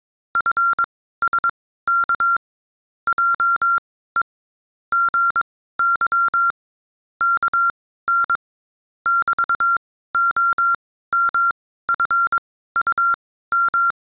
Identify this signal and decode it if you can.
Morse code, heard as a keyed tone, about 22 wpm, spaced as for 15 wpm, 1380 Hz. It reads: FSK JE GY KN <BT>OMFUM